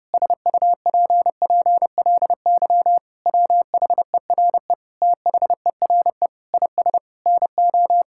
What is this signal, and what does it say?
Morse code, heard as a keyed tone, about 30 wpm, 700 Hz. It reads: SUPPLY WHERE THERE IS NO
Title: Sentences from Top 1000 Words - Review Part 5 - 30wpm